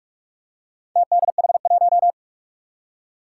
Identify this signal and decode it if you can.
Morse code, heard as a keyed tone, about 46 wpm, 700 Hz. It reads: TDH1
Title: Four Challenging Letter-Number Combinations 45wpm